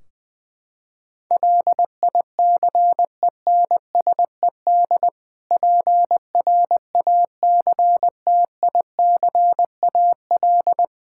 Morse code, heard as a keyed tone, about 20 wpm, 700 Hz. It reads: LICENSED PRACTICAL
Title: Practice QSO - Exchange 3 - More Info - 20wpm